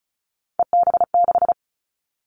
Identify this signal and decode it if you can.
Morse code, heard as a keyed tone, about 36 wpm, 700 Hz. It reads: EB6
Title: Callsign Prefixes for Top 50 DXCC - 35wpm